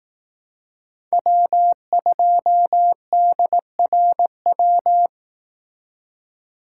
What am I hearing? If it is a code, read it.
Morse code, heard as a keyed tone, about 18 wpm, 700 Hz. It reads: W2DRW
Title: Speed Racing - USA 1x3 Call Signs 15wpm